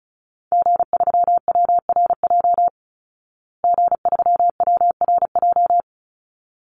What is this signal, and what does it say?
Morse code, heard as a keyed tone, about 34 wpm, 700 Hz. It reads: G3WRJ G3WRJ
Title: Call Signs Repeated 3x - 35wpm